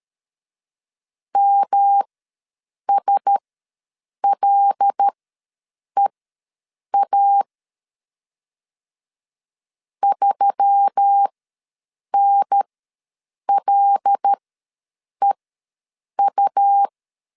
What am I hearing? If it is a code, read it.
Morse code, heard as a keyed tone, about 13 wpm, spaced as for 7 wpm, 800 Hz. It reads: MSLEA 3NLEU
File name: LEZIONI / 5 LBVC / 5 L ws 12wpm 2958